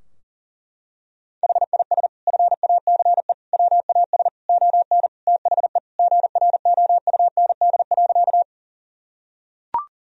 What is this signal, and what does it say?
Morse code, heard as a keyed tone, about 40 wpm, 700 Hz. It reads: HIS FACE WAS ON THE GROUND.
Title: Sentences from Top 1000 Words - Review Part 2 - 40wpm